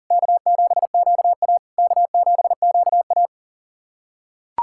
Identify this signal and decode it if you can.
Morse code, heard as a keyed tone, about 37 wpm, 690 Hz. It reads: K7QA K7QA
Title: SOTA QSO Practice 40wpm